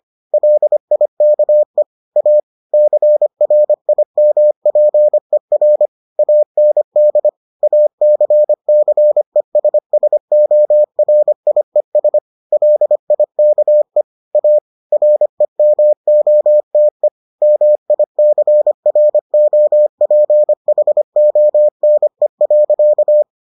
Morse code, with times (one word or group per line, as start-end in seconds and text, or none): 0.3-1.8 LIKE
2.2-2.4 A
2.7-5.9 CRIMPER
6.2-7.3 AND
7.6-12.2 ACCESSORIES
12.5-14.0 LIKE
14.3-14.6 A
14.9-17.1 REMOTE
17.4-23.2 MICROPHONE.